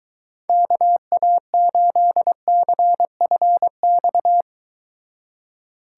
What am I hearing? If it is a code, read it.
Morse code, heard as a keyed tone, about 23 wpm, 700 Hz. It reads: KA8CFX